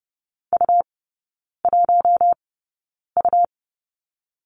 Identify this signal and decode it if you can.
Morse code, heard as a keyed tone, about 30 wpm, spaced as for 9 wpm, 700 Hz. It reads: U1U
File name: Lesson 18e - Letter m - Three Characters 30wpm